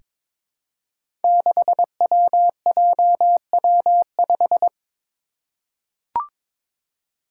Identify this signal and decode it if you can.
Morse code, heard as a keyed tone, about 23 wpm, 700 Hz. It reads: 6WJW5
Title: Five Challenging Letter-Number Combinations 22wpm